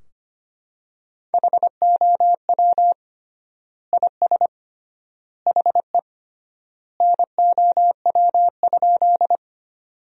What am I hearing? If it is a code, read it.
Morse code, heard as a keyed tone, about 25 wpm, 700 Hz. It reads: HOW IS HE NOW?
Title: 3x Word-Spacing - Sentences from Top 100 Words - 25wpm